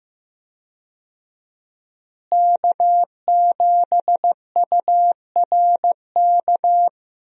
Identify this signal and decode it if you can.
Morse code, heard as a keyed tone, about 15 wpm, 700 Hz. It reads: K7URK
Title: USA Call Signs - 5 Characters - Repeated 2x - 15wpm